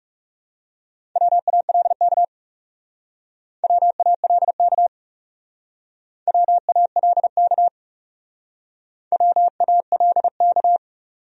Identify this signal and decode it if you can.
Morse code, heard as a keyed tone, about 38 wpm, 700 Hz. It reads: WALK WALK WALK WALK